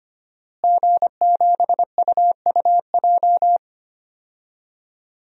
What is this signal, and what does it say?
Morse code, heard as a keyed tone, about 25 wpm, 700 Hz. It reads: G7UUJ
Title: DXCC-100 Call Signs - 5 Characters - Repeated 2x - 25wpm